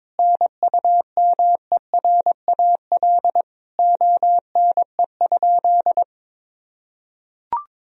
Morse code, heard as a keyed tone, about 22 wpm, 700 Hz. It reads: NUMERAL ONE?
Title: Sentences from Top 1000 Words - Review Part 2 - 22wpm